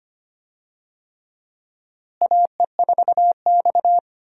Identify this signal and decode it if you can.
Morse code, heard as a keyed tone, about 25 wpm, 700 Hz. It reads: AE4X
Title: USA Call Signs - 4 Characters - Repeated 3x - 25wpm